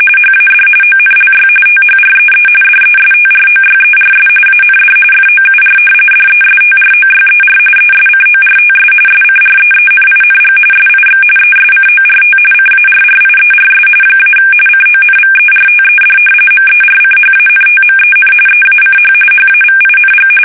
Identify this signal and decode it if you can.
RTTY, about 45.45 baud. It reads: HAD ONLY THREE DESTROYERS AND FOUR DESTROYER ESCORTS SCREENING SIX ESCORT CARRIERS - CONVERTED MERCHANTMEN EACH HAVING
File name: fsk850 / battleshipnj